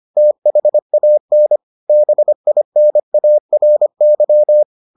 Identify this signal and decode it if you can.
Morse code, heard as a keyed tone, about 25 wpm, 600 Hz. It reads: THAN BINARY